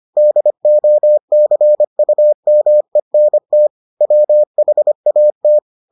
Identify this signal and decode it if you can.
Morse code, heard as a keyed tone, about 25 wpm, 600 Hz. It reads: DOCUMENT WHAT